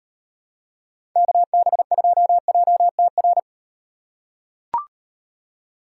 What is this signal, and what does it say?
Morse code, heard as a keyed tone, about 35 wpm, 700 Hz. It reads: KB2JTR